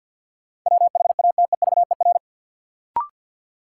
Morse code, heard as a keyed tone, about 53 wpm, 700 Hz. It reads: WHATEVER
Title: Sentences from Top 1100 Words - New Word Drill - 50wpm